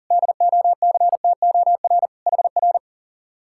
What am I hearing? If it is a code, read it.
Morse code, heard as a keyed tone, about 47 wpm, 700 Hz. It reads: DOCTOR HR
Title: Speed-Racing Common QSO Phrases - 40wpm